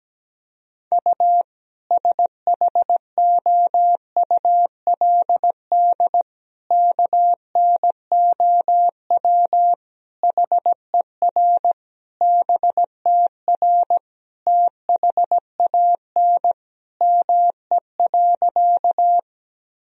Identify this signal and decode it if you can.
Morse code, heard as a keyed tone, about 17 wpm, 700 Hz. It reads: U SHOULD KNOW HER BTR THAN ME.
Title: Sentences from Top 400 Words and Abbreviations 17wpm